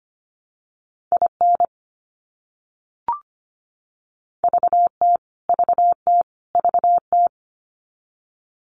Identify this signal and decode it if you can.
Morse code, heard as a keyed tone, about 25 wpm, 700 Hz. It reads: IN VT VT VT
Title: US States Repeated 3x 25wpm